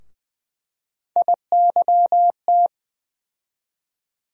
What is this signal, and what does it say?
Morse code, heard as a keyed tone, about 20 wpm, 700 Hz. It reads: IYT
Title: USA Call Sign Postfixes - 3 Letters 20wpm